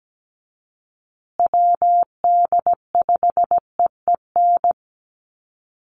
Morse code, heard as a keyed tone, about 17 wpm, 700 Hz. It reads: WD5EEN
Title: USA Call Signs Repeated 3x 17wpm